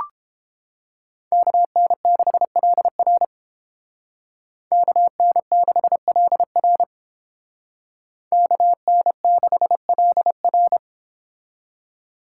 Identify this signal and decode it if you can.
Morse code, heard as a keyed tone, about 30 wpm, 700 Hz. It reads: KN6LR KN6LR KN6LR